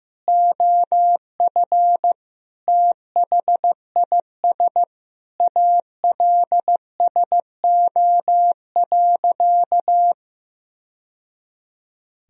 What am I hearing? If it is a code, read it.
Morse code, heard as a keyed tone, about 15 wpm, 700 Hz. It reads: OF THIS ALSO.
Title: Sentences from Top 1000 Words - Review Part 1 - 15wpm